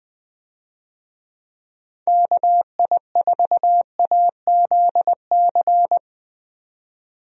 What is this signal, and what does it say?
Morse code, heard as a keyed tone, about 20 wpm, 700 Hz. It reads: KI4AZC
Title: USA Call Signs - 6 Characters - Repeated 3x - 20wpm